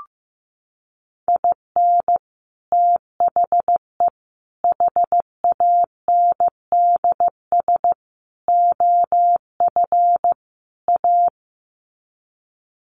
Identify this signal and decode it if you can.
Morse code, heard as a keyed tone, about 15 wpm, 700 Hz. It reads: IN THE HANDS OF A
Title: Sets of 5 Words Encore 15wpm